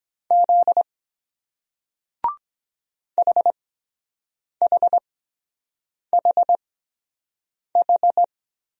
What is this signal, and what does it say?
Morse code, heard as a keyed tone, about 23 wpm, spaced as for 7 wpm, 700 Hz. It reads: Z HHHH